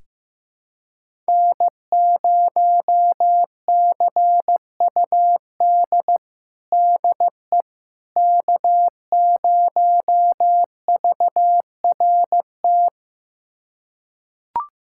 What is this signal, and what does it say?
Morse code, heard as a keyed tone, about 15 wpm, 700 Hz. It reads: N0CUD DE K0VRT
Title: USA Callsign DE Callsign - 5 Characters - 15wpm